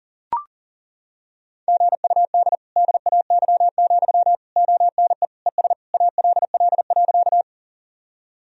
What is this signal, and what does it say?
Morse code, heard as a keyed tone, about 40 wpm, 700 Hz. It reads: GUD DAY, ONE ES ALL.
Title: Sentences from Top 200 Words and Abbreviations 40wpm